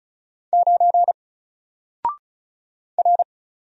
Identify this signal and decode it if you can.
Morse code, heard as a keyed tone, about 35 wpm, 700 Hz. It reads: 9 R
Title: Single Character (Letter:Number:Punctuation:Prosign) 35wpm